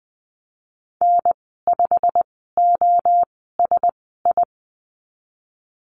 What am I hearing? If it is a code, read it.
Morse code, heard as a keyed tone, about 20 wpm, spaced as for 15 wpm, 700 Hz. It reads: N5OSI
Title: 088 Lesson 15g - Number 5 - Call Signs 20wpm